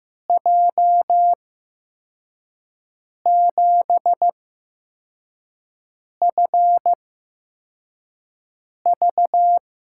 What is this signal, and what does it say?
Morse code, heard as a keyed tone, about 15 wpm, spaced as for 4 wpm, 700 Hz. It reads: J7FV
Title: USA Call Signs - 8x Character Spacing - 15wpm